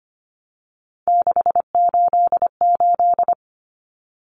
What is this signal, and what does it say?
Morse code, heard as a keyed tone, about 25 wpm, 700 Hz. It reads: 688
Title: Numbers - 3-Digit 25wpm